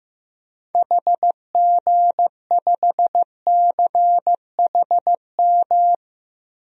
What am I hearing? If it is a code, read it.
Morse code, heard as a keyed tone, about 15 wpm, 700 Hz. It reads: HG5CHM
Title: DXCC-100 Call Signs - 6 Characters - Repeated 3x - 15wpm